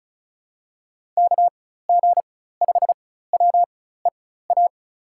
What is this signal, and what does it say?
Morse code, heard as a keyed tone, about 36 wpm, spaced as for 16 wpm, 700 Hz. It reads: KG5WEA
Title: USA Call Signs - 6 Characters - 4x Character Spacing - 35wpm